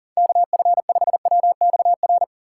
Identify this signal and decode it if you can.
Morse code, heard as a keyed tone, about 37 wpm, 700 Hz. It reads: KF5WXR
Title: USA Call Signs - 6 Characters - 4x Character Spacing - 40wpm